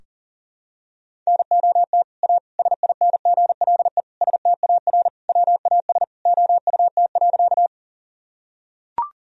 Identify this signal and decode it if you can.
Morse code, heard as a keyed tone, about 40 wpm, 700 Hz. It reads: NOT A SINGLE STAR WAS OUT.
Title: Sentences from Top 1000 Words - Review Part 4 - 40wpm